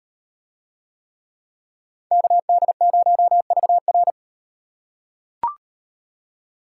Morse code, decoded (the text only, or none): KD0VR